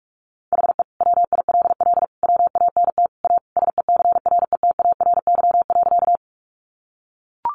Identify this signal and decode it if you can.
Morse code, heard as a keyed tone, about 44 wpm, 700 Hz. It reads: HE WILL WANT A SECRETARY.
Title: Sentences from Top 1300 Words 45wpm